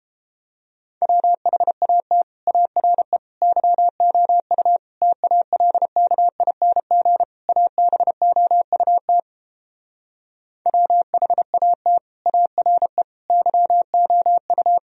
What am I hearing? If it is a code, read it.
Morse code, heard as a keyed tone, about 31 wpm, 700 Hz. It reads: WHAT ARE YOU TALKING ABOUT WHAT ARE YOU